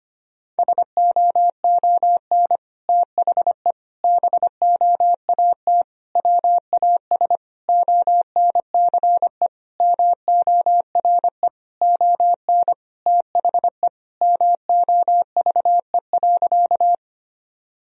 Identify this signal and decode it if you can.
Morse code, heard as a keyed tone, about 25 wpm, 700 Hz. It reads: SOON THE BOAT WAS ONCE MORE ON THE MOVE.
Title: Sentences from Top 1000 Words - Review Part 3 - 25wpm